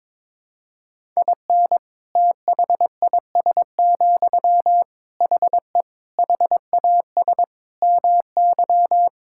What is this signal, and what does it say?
Morse code, heard as a keyed tone, about 22 wpm, 700 Hz. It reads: IN THIS, HE HAS MY
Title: Sentences from Top 1000 Words - Review Part 5 - 22wpm